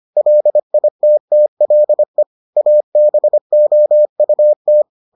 Morse code, heard as a keyed tone, about 25 wpm, 600 Hz. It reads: LITTLE ABOUT